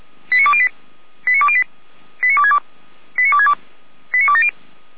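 Five-tone selective call (ZVEI), heard as quick runs of five tones, tones of about 70 ms. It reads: 89208 89208 89262 89262 89270
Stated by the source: ZVEI five-tone call received on SDR.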